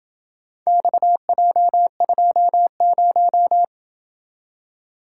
Morse code, heard as a keyed tone, about 26 wpm, 700 Hz. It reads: XJ20